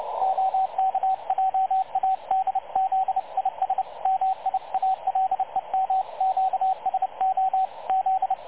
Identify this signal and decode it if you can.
Morse code, heard as a keyed tone, about 30 wpm, 760 Hz, with radio noise. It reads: OK JA DZIS MIALEM QSO Z
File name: dlazaawansowanych / nagraniazpasma / websdr recording 3567.0kHz